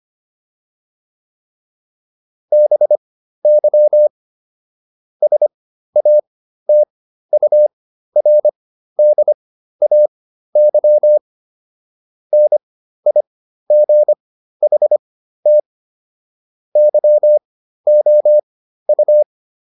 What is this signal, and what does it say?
Morse code, heard as a keyed tone, about 25 wpm, spaced as for 13 wpm, 600 Hz. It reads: BY SATURDAY NIGHT YOU